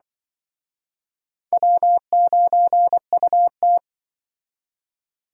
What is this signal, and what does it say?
Morse code, heard as a keyed tone, about 26 wpm, 700 Hz. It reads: W9UT